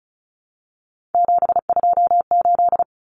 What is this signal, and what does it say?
Morse code, heard as a keyed tone, about 34 wpm, 700 Hz. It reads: 728
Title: Common Cut Numbers - 3-Digit 35wpm